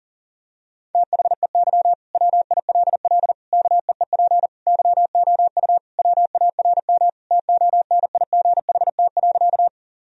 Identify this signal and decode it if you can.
Morse code, heard as a keyed tone, about 40 wpm, 700 Hz. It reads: THEY WILL KEEP YOU WARM TONIGHT.
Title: Sentences from Top 1000 Words - Review Part 3 - 40wpm